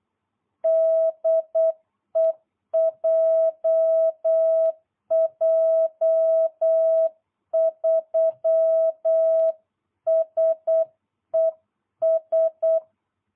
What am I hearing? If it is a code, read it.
Morse code, heard as a keyed tone, about 8 wpm, 660 Hz. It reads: DE JJ3SES